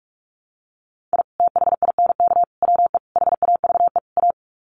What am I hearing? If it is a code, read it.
Morse code, heard as a keyed tone, about 44 wpm, 700 Hz. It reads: I THINK WE HAVE A
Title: Sets of 5 Words Encore 45wpm